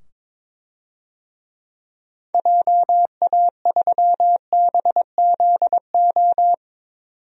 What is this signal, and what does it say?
Morse code, heard as a keyed tone, about 22 wpm, 700 Hz. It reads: JA3BZO
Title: DXCC-100 Call Signs - 6 Characters - 22wpm